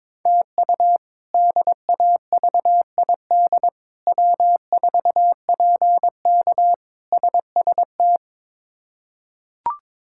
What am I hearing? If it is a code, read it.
Morse code, heard as a keyed tone, about 22 wpm, 700 Hz. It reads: TU DAVID W4PK SST
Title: K1USN SST QSO Practice - 22wpm